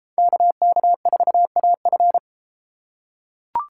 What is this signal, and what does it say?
Morse code, heard as a keyed tone, about 32 wpm, 700 Hz. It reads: KK4AF